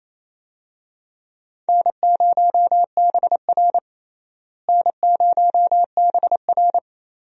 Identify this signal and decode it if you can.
Morse code, heard as a keyed tone, about 28 wpm, 700 Hz. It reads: N0BR N0BR
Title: USA Call Signs - 4 Characters - Repeated 3x - 28wpm